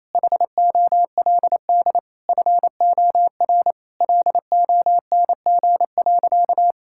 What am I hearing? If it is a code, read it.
Morse code, heard as a keyed tone, about 27 wpm, 700 Hz. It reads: HOLD FOR LONG.
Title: Sentences from Top 1000 Words - Review Part 5 - 28wpm